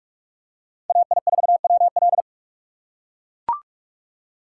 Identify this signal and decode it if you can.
Morse code, heard as a keyed tone, about 39 wpm, 700 Hz. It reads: AI4WL